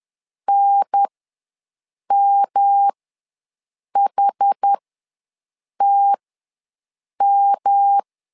Morse code, heard as a keyed tone, about 10 wpm, spaced as for 6 wpm, 800 Hz. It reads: NMHTM
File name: LEZIONI / 3 MWRO / 3 M ws 10wpm 2121